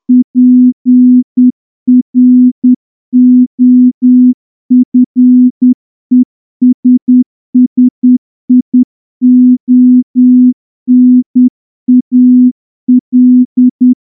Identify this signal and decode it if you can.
Morse code, heard as a keyed tone, about 10 wpm, 250 Hz. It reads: PROFESSIONAL
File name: morse-training-10WPM